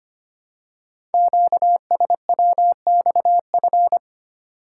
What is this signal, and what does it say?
Morse code, heard as a keyed tone, about 25 wpm, 690 Hz. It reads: QSWXF
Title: Code Groups - 5 Letters - 25wpm